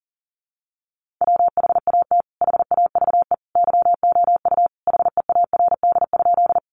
Morse code, heard as a keyed tone, about 40 wpm, 700 Hz. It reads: WHAT HAVE YOU HEARD?
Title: Sentences from Top 400 Words 40wpm